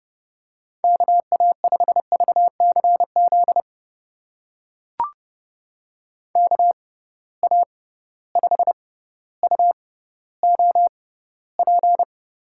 Morse code, heard as a keyed tone, about 32 wpm, spaced as for 16 wpm, 700 Hz. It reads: KA5VCZ KA5UOP
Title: USA Call Signs - 6 Characters - 6x Character Spacing - 30wpm